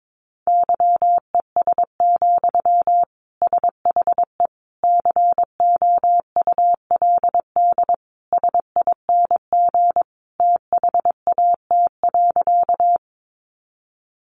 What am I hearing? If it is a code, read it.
Morse code, heard as a keyed tone, about 22 wpm, 700 Hz. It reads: YES, SHE COULD SING THAT.
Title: Sentences from Top 500 Words 22wpm